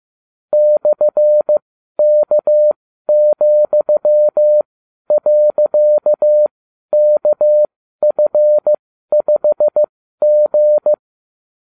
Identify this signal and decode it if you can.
Morse code, heard as a keyed tone, about 15 wpm, spaced as for 12 wpm, 600 Hz. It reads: /K,.KF5G